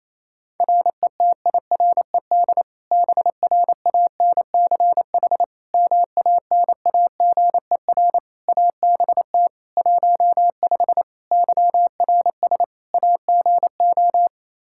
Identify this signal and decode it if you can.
Morse code, heard as a keyed tone, about 28 wpm, 700 Hz. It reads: RETIRED BRANCH MANAGER ABT 15 YRS AGO
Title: Practice QSO - Exchange 3 - More Info - 28wpm